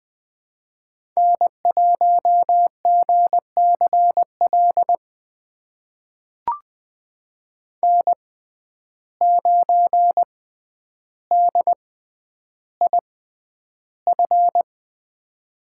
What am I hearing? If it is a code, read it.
Morse code, heard as a keyed tone, about 20 wpm, spaced as for 10 wpm, 700 Hz. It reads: N1GCL N9DIF
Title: USA Call Signs - 5 Characters - 6x Character Spacing - 20wpm